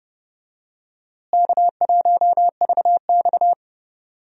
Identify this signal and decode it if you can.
Morse code, heard as a keyed tone, about 31 wpm, 700 Hz. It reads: K1VX
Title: USA Call Signs - 4 Characters - Repeated 3x - 30wpm